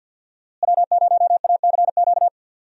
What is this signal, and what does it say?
Morse code, heard as a keyed tone, about 53 wpm, 700 Hz. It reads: W0ACQ
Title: USA Call Signs - 5 Characters - Repeated 3x - 50wpm